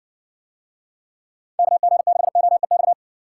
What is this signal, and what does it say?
Morse code, heard as a keyed tone, about 59 wpm, 700 Hz. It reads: KG6OEX